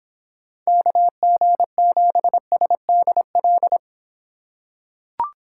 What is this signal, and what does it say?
Morse code, heard as a keyed tone, about 23 wpm, 700 Hz. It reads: KG7SDL